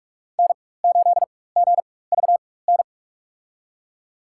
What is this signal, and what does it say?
Morse code, heard as a keyed tone, about 47 wpm, spaced as for 21 wpm, 700 Hz. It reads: N8GVN
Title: USA Call Signs - 5 Characters - 4x Character Spacing - 45wpm